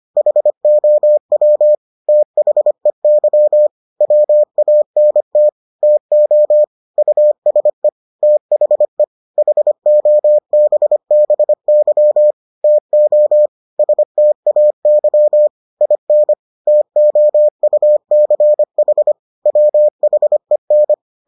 Morse code, heard as a keyed tone, about 25 wpm, 600 Hz. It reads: HOW THEY WANT TO USE THE HOBBY TO STAY IN TOUCH WHEN